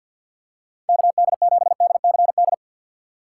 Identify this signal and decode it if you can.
Morse code, heard as a keyed tone, about 53 wpm, 700 Hz. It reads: KD7DCD
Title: USA Call Signs - 3x Character Spacing - 50wpm